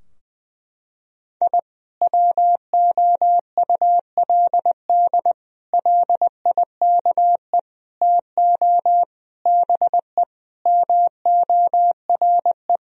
Morse code, heard as a keyed tone, about 21 wpm, 700 Hz. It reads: I WOULD LIKE TO BE MORE
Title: Sentences from Top 1000 Words - Review Part 3 - 20wpm